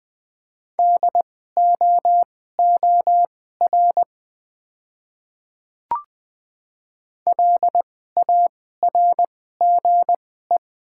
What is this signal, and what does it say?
Morse code, heard as a keyed tone, about 20 wpm, spaced as for 14 wpm, 700 Hz. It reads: DOOR LARGE
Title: 2x Character Spacing - Top 300 Words - 20wpm